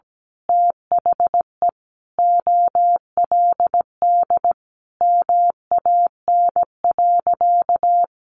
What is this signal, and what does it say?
Morse code, heard as a keyed tone, about 17 wpm, 700 Hz. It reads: THE OLD MAN.
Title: Sentences from Top 500 Words 17wpm